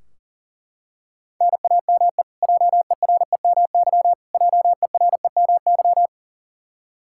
Morse code, heard as a keyed tone, about 40 wpm, 700 Hz. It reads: NAME JEREMY JEREMY
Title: Practice QSO - Exchange 1 - Beginning - 40wpm